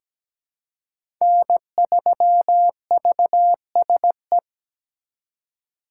Morse code, heard as a keyed tone, about 17 wpm, 700 Hz. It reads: N3VSE
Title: USA Call Signs - 5 Characters - 3x Character Spacing - 17wpm